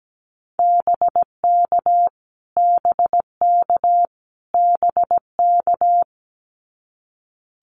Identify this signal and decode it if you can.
Morse code, heard as a keyed tone, about 17 wpm, 700 Hz. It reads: BK BK BK
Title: Abbreviations Repeated 3x - 17wpm